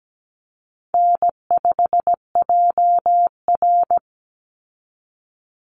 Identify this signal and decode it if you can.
Morse code, heard as a keyed tone, about 17 wpm, 700 Hz. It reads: N5JR
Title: Easy Call Signs Repeated 3x - 17wpm